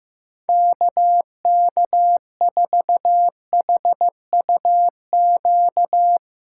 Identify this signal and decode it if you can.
Morse code, heard as a keyed tone, about 15 wpm, 700 Hz. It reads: KK4HUQ
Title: USA Call Signs - 6 Characters - 4x Character Spacing - 15wpm